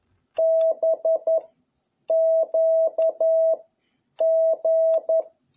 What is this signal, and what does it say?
Morse code, heard as a keyed tone, about 11 wpm, 650 Hz. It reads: B Q G
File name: jh8cbh / 2021cw